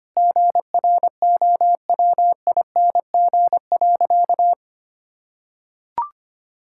Morse code, heard as a keyed tone, about 25 wpm, 700 Hz. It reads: GROWING.
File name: practice / 25wpm / Sentences with 2-Word Phrases - 25wpm